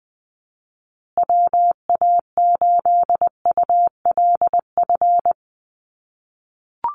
Morse code, heard as a keyed tone, about 20 wpm, 700 Hz. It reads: WA8ULF
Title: USA Call Signs Repeated 2x 20wpm